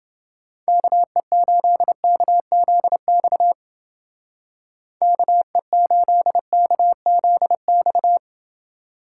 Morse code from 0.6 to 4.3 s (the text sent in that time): KE8KZX